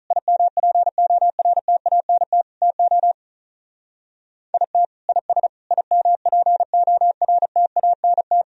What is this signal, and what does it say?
Morse code, heard as a keyed tone, about 44 wpm, 700 Hz. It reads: IMPORTANT TO IT IS IMPORTANT